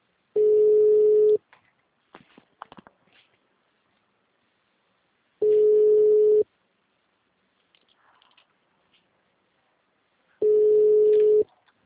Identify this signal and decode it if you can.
Ringback tone (Europe, 425 Hz).
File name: out- -2013- - - .4927